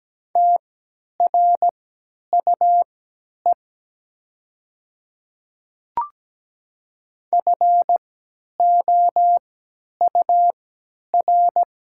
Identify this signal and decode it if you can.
Morse code, heard as a keyed tone, about 17 wpm, spaced as for 10 wpm, 700 Hz. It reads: TRUE FOUR
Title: 3x Character Spacing - Top 500 Words - 17wpm